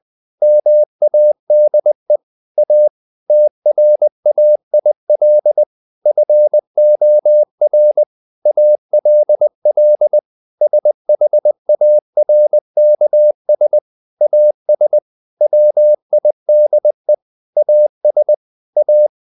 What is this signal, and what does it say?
Morse code, heard as a keyed tone, about 20 wpm, 600 Hz. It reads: MADE A TRAIL FOR ALL SHARKS AS WIDE AS A
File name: the old man and the sea part 2 20wpm